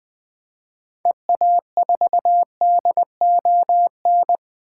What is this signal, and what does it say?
Morse code, heard as a keyed tone, about 20 wpm, 700 Hz. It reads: EA4DON
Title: 2023 SOTA Activator Call Signs - 20wpm